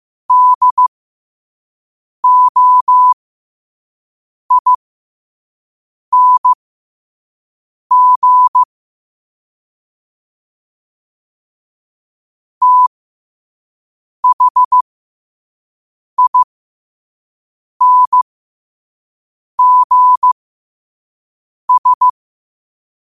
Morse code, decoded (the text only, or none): DOING THINGS